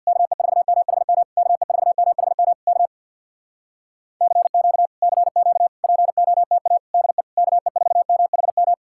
Morse code, heard as a keyed tone, about 65 wpm, 700 Hz. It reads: KE4MHM KE4MHM K CQ CQ POTA DE KE4MHM